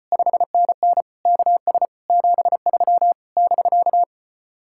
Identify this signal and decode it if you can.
Morse code, heard as a keyed tone, about 33 wpm, 700 Hz. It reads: 5NN KS 73 <BK>